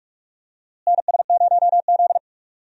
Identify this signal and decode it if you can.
Morse code, heard as a keyed tone, about 42 wpm, 700 Hz. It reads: NS0Z
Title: USA Call Signs - 4 Characters - Repeated 3x - 45wpm